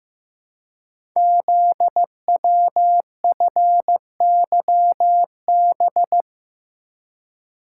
Morse code, heard as a keyed tone, about 15 wpm, 700 Hz. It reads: ZWFYB
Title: Code Groups - 5 Letters - 15wpm